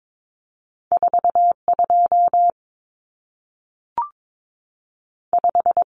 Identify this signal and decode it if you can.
Morse code, heard as a keyed tone, about 22 wpm, 700 Hz. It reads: 4 2 5
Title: Numbers - 2-Digit 22wpm